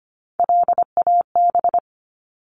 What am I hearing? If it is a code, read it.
Morse code, heard as a keyed tone, about 25 wpm, 700 Hz. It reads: LAB
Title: Top 1751-2000 Words 25wpm